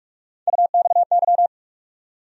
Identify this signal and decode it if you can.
Morse code, heard as a keyed tone, about 46 wpm, 700 Hz. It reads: UXY
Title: USA Call Sign Postfixes - 3 Letters 45wpm